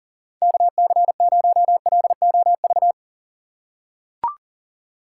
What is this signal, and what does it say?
Morse code, heard as a keyed tone, about 40 wpm, 700 Hz. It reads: KC0LOV
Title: USA Call Signs - 6 Characters - 3x Character Spacing - 40wpm